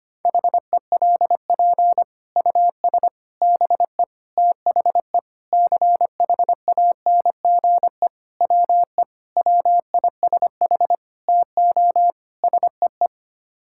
Morse code, heard as a keyed tone, about 25 wpm, 700 Hz. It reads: HELP US BE THE CHANGE WE WISH TO SEE
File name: Sets of 9 Words - 25wpm